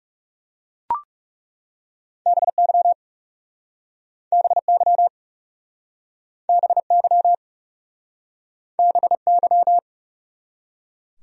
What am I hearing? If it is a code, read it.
Morse code, heard as a keyed tone, about 40 wpm, 700 Hz. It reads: BY BY BY BY